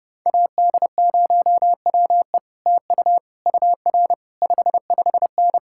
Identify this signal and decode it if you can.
Morse code, heard as a keyed tone, about 32 wpm, 700 Hz. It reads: AD0WE TU UR 55N